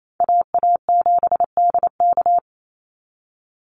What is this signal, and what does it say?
Morse code, heard as a keyed tone, about 28 wpm, 700 Hz. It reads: AA7DK
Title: NA SOTA Chaser Callsigns 28wpm